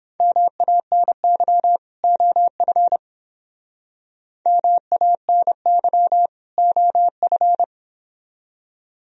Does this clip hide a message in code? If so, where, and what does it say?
Yes: Morse code from 0.2 to 7.6 s, heard as a keyed tone, about 30 wpm, 700 Hz. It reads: MANY OF MANY OF